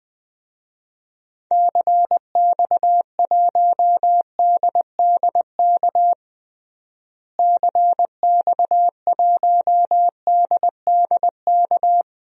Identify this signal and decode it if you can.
Morse code, heard as a keyed tone, about 21 wpm, 700 Hz. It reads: CX1DDK CX1DDK
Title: DXCC-100 Call Signs - 6 Characters - Repeated 3x - 20wpm